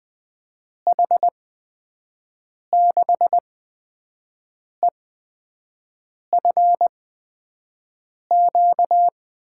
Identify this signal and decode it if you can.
Morse code, heard as a keyed tone, about 20 wpm, spaced as for 5 wpm, 700 Hz. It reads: H6EFQ
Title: USA Call Signs - 8x Character Spacing - 20wpm